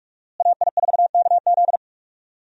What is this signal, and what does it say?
Morse code, heard as a keyed tone, about 41 wpm, 700 Hz. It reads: AI4KZ
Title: USA Call Signs - 5 Characters - Repeated 3x - 45wpm